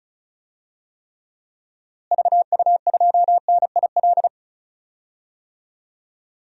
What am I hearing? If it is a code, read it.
Morse code, heard as a keyed tone, about 34 wpm, 700 Hz. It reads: VU2NIL
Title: DXCC-100 Call Signs - 6 Characters - Repeated 2x - 35wpm